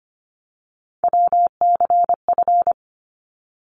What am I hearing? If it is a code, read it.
Morse code, heard as a keyed tone, about 25 wpm, 700 Hz. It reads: WCF
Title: ARRL Sections 25wpm